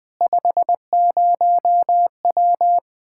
Morse code, heard as a keyed tone, about 20 wpm, 700 Hz. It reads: 50W
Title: Speed-Racing Common QSO Phrases Encore - 17wpm